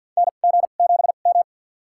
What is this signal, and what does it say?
Morse code, heard as a keyed tone, about 49 wpm, spaced as for 36 wpm, 700 Hz. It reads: NG7M
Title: USA Call Signs - 4 Characters - 2x Character Spacing - 50wpm